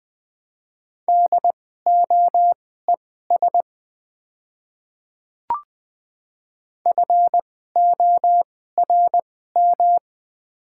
Morse code, heard as a keyed tone, about 20 wpm, spaced as for 15 wpm, 700 Hz. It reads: DOES FORM
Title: 2x Character Spacing - Top 200 Words - 20wpm